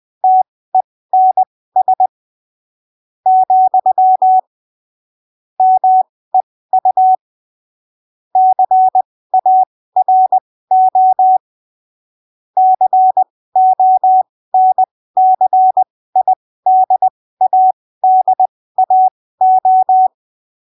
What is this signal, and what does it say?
Morse code, heard as a keyed tone, about 20 wpm, spaced as for 15 wpm, 750 Hz. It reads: TENS , MEU CARO CONCIDADAO
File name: arquivos / texto / texto 26 20ppm